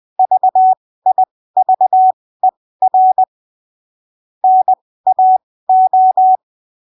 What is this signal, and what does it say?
Morse code, heard as a keyed tone, about 20 wpm, spaced as for 15 wpm, 750 Hz. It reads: VIVER NAO